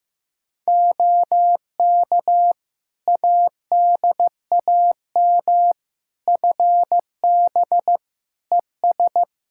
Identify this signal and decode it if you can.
Morse code, heard as a keyed tone, about 15 wpm, 700 Hz. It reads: OK ADAM FB ES
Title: Practice QSO - Exchange 2 - Info - 15wpm